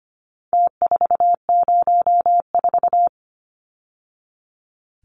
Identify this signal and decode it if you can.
Morse code, heard as a keyed tone, about 25 wpm, 700 Hz. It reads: T404
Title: Numbers - Contest Style - 4-Digit 25wpm